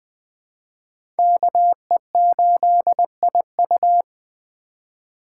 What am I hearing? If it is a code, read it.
Morse code, heard as a keyed tone, about 20 wpm, 700 Hz. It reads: KE8IU
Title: Speed Racing - USA 2x2 Call Signs 17wpm